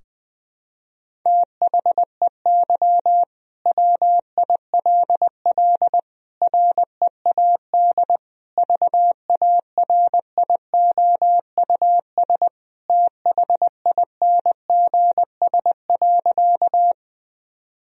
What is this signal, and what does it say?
Morse code, heard as a keyed tone, about 20 wpm, 700 Hz. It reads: THEY WILL READ VARIOUS THINGS.
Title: Sentences from Top 1000 Words - Review Part 4 - 20wpm